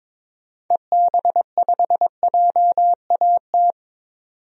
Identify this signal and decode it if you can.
Morse code, heard as a keyed tone, about 22 wpm, 700 Hz. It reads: EB5JAT
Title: DXCC-100 Call Signs - 6 Characters - 22wpm